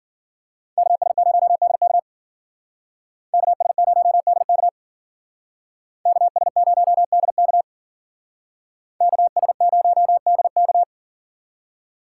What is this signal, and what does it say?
Morse code, heard as a keyed tone, about 52 wpm, 700 Hz. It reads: KS0DK KS0DK KS0DK KS0DK